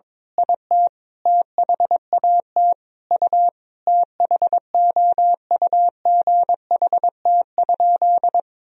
Morse code, heard as a keyed tone, about 22 wpm, 700 Hz. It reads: IT THAT U THOUGHT?
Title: Sentences from Top 300 Words and Abbreviations 22wpm